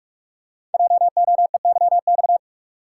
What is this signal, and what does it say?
Morse code, heard as a keyed tone, about 44 wpm, 690 Hz. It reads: JOEYX
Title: Code Groups - 5 Letter:Numbers - 45wpm